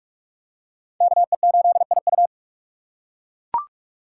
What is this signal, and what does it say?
Morse code, heard as a keyed tone, about 38 wpm, 700 Hz. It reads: KE8IU